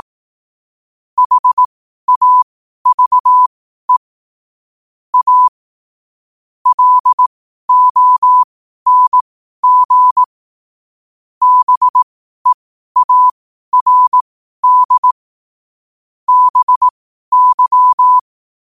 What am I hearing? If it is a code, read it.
Morse code, heard as a keyed tone, about 18 wpm, spaced as for 12 wpm, 1000 Hz. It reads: HAVE A LONG BEARD BY
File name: qotdfiles / 12wpm